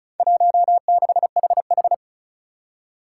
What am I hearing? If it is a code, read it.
Morse code, heard as a keyed tone, about 36 wpm, 700 Hz. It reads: 16HH
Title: Four Challenging Letter-Number Combinations 35wpm